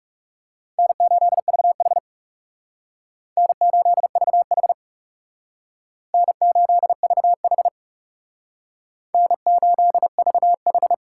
Morse code, heard as a keyed tone, about 38 wpm, 700 Hz. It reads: N8VH N8VH N8VH N8VH